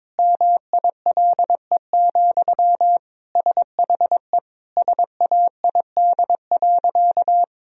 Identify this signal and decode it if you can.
Morse code, heard as a keyed tone, about 22 wpm, 700 Hz. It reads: MILE, SHE SAID.
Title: Sentences from Top 1000 Words - Review Part 1 - 22wpm